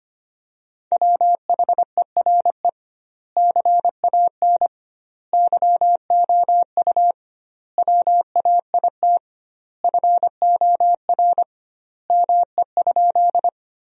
Morse code, heard as a keyed tone, about 25 wpm, 700 Hz. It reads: WHERE CAN YOU WAIT FOR ME?
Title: 2x Word-Spacing - Sentences from Top 500 Words - 25wpm